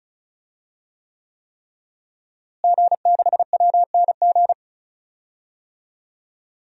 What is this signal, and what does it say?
Morse code, heard as a keyed tone, about 34 wpm, 700 Hz. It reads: G6WNG
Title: DXCC-100 Call Signs - 5 Characters - Repeated 3x - 35wpm